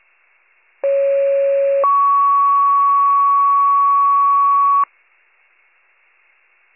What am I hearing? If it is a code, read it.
Two-tone page, heard as a one-second tone, then a three-second tone, about 550 then 1120 Hz. It is heard from 0.8 to 4.8 s.